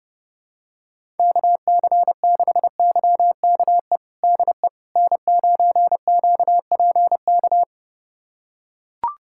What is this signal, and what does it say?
Morse code, heard as a keyed tone, about 31 wpm, 700 Hz. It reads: KC6YKE DE N9QPK
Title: USA Callsign DE Callsign 30wpm